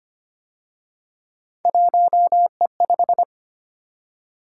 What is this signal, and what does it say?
Morse code, heard as a keyed tone, about 29 wpm, 700 Hz. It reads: 1E5
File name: Speed-Racing - Three Letter-Number Combinations - 25wpm